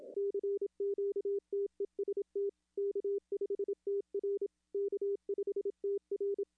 Morse code, heard as a keyed tone, about 27 wpm, 400 Hz. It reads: CQ TEST K5TR K5TR
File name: contests / arrl-160-k5tr-op-w5tn -0600